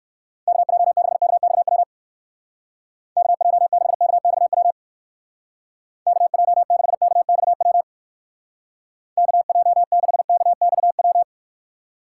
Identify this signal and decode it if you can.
Morse code, heard as a keyed tone, about 56 wpm, 700 Hz. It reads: KJ6KXW KJ6KXW KJ6KXW KJ6KXW
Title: Speed Racing - USA 2x3 Call Signs 45wpm